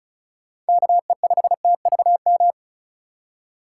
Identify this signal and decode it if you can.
Morse code, heard as a keyed tone, about 36 wpm, 700 Hz. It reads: KE5TVM
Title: USA Call Signs - 4x Character Spacing - 35wpm